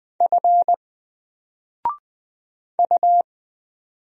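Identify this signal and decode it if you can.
Morse code, heard as a keyed tone, about 20 wpm, 700 Hz. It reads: F U
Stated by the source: Single letter and number Morse drill at 20 wpm, from N4LCD.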